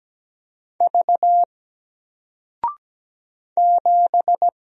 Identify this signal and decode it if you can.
Morse code, heard as a keyed tone, about 17 wpm, 700 Hz. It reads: V 7
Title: Single Letter-Number Alt 17wpm